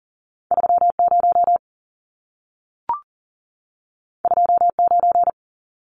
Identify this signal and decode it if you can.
Morse code, heard as a keyed tone, about 40 wpm, 700 Hz. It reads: 30 29
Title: Numbers - 2-Digit 40wpm